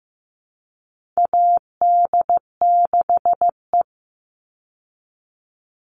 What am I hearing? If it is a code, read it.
Morse code, heard as a keyed tone, about 15 wpm, 700 Hz. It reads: AD6E
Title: Call Signs Repeated 3x - 15wpm